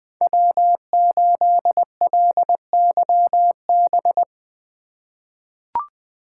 Morse code, heard as a keyed tone, about 20 wpm, 700 Hz. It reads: W8LYB
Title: USA Call Signs - 5 Characters - 3x Character Spacing - 20wpm